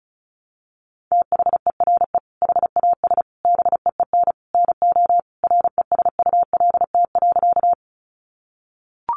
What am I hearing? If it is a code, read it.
Morse code, heard as a keyed tone, about 34 wpm, 700 Hz. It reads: THERE HAS BEEN NO RESULT.
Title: Sentences from Top 700 Words 35wpm